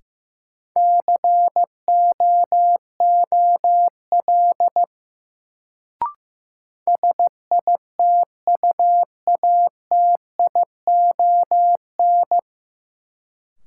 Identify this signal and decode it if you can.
Morse code, heard as a keyed tone, about 15 wpm, 700 Hz. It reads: COOL SITUATION
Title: Sentences from Top 1200 Words - New Word Drill - 15wpm